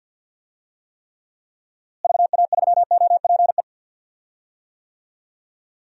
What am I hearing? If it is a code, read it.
Morse code, heard as a keyed tone, about 50 wpm, 700 Hz. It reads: VA3OPE
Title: DXCC-100 Call Signs - 6 Characters - Repeated 2x - 50wpm